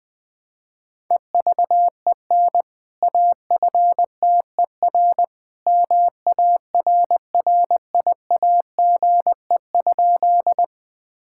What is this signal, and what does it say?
Morse code, heard as a keyed tone, about 20 wpm, 700 Hz. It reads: EVEN AFTER MARRIAGE?
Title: Sentences from Top 1000 Words - Review Part 5 - 20wpm